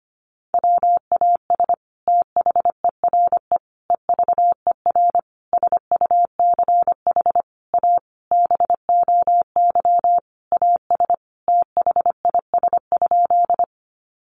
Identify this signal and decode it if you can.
Morse code, heard as a keyed tone, about 25 wpm, 700 Hz. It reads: WAS THERE EVER SUCH A BOY AS THIS?
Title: Sentences from Top 400 Words 25wpm